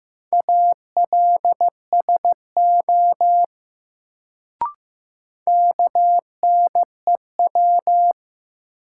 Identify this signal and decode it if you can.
Morse code, heard as a keyed tone, about 15 wpm, 700 Hz. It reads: ALSO KNEW
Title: Sentences from Top 400 Words - New Word Drill - 15wpm